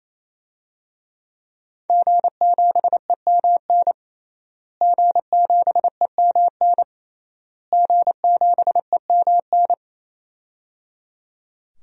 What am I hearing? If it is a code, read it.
Morse code, heard as a keyed tone, about 27 wpm, 700 Hz. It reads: G7EMN G7EMN G7EMN
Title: DXCC-100 Call Signs - 5 Characters - Repeated 3x - 28wpm